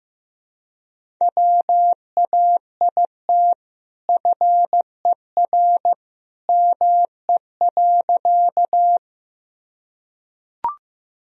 Morse code, heard as a keyed tone, about 16 wpm, 700 Hz. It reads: WAIT FER ME.
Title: Sentences from Top 500 Words and Abbreviations 15wpm